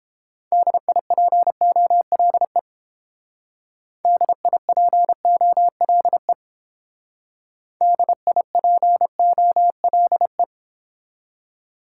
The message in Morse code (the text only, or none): DIPOLE DIPOLE DIPOLE